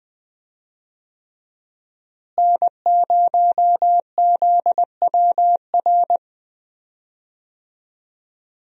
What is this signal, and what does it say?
Morse code, heard as a keyed tone, about 20 wpm, 700 Hz. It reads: N0ZWR
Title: USA Call Signs - 5 Characters - Repeated 3x - 20wpm